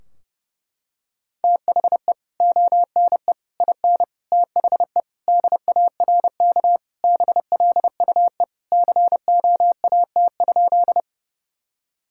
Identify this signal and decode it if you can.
Morse code, heard as a keyed tone, about 30 wpm, 700 Hz. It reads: THE ONE IN THE DARK BLUE COAT?
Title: Sentences from Top 1600 Words 30wpm